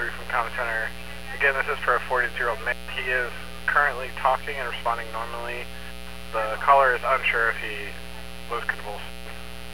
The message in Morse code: none

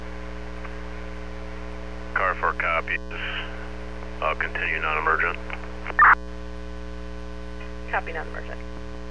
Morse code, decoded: none

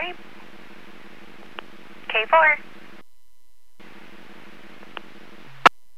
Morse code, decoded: none

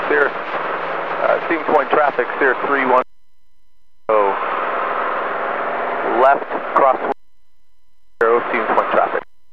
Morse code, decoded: none

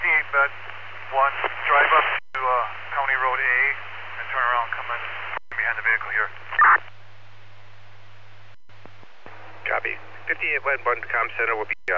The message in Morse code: none

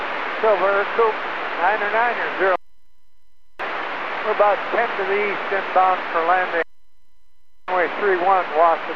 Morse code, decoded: none

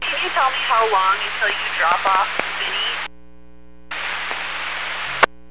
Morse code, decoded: none

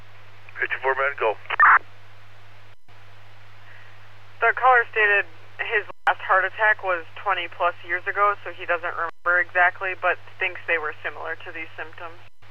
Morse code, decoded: none

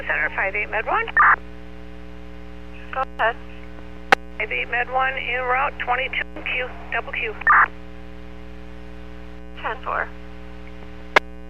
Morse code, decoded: none

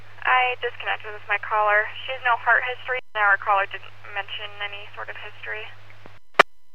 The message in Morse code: none

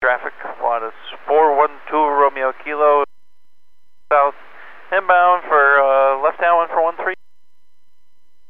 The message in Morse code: none

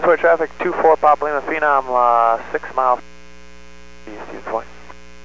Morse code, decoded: none

